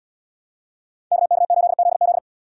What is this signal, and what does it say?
Morse code, heard as a keyed tone, about 70 wpm, 700 Hz. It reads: KK8ZZ